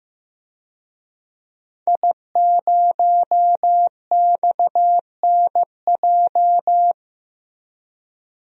Morse code, heard as a keyed tone, about 15 wpm, 700 Hz. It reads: I0XNJ
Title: DXCC-100 Call Signs - 5 Characters - Repeated 2x - 15wpm